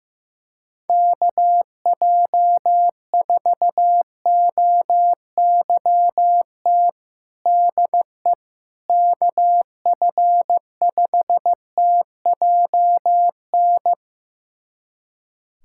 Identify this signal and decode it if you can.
Morse code, heard as a keyed tone, about 16 wpm, 700 Hz. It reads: KJ4OYT DE KF5TJN
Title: USA Callsign DE Callsign - 6 Characters - 15wpm